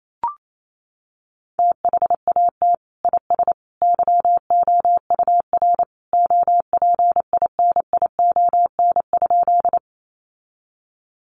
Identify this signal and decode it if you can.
Morse code, heard as a keyed tone, about 28 wpm, 700 Hz. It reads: THAT IS YOUR OPINION?
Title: Sentences from Top 1200 Words 28wpm